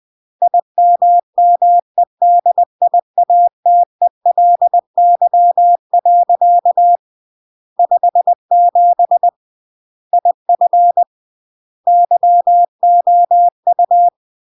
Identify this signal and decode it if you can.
Morse code, heard as a keyed tone, about 20 wpm, 700 Hz. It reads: IMMEDIATELY. 57 IF YOU
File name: CW / jodorowsky / 82 commandments 20-20